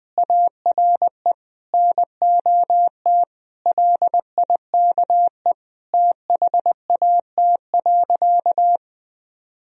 Morse code, heard as a keyed tone, about 20 wpm, 700 Hz. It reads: ARE NOT LIKE THAT.
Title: Sentences from Top 1000 Words - Review Part 3 - 20wpm